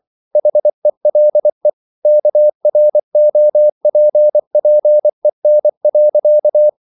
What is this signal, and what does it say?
Morse code, heard as a keyed tone, about 24 wpm, 600 Hz. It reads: HELE KROPPEN.
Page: Ferie 24 wpm